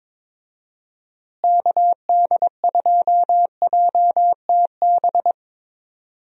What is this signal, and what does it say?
Morse code, heard as a keyed tone, about 22 wpm, 700 Hz. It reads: KD2JTB
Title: USA Call Signs - 6 Characters - Repeated 3x - 22wpm